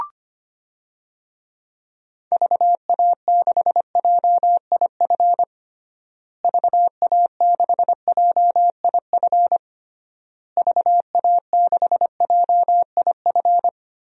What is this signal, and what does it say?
Morse code, heard as a keyed tone, about 25 wpm, 700 Hz. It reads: VA6JIF VA6JIF VA6JIF
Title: DXCC-100 Call Signs - 6 Characters - Repeated 3x - 25wpm